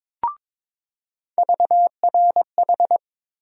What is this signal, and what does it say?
Morse code, heard as a keyed tone, about 22 wpm, 700 Hz. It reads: VRH
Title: USA Call Sign Postfixes - 3 Letters 22wpm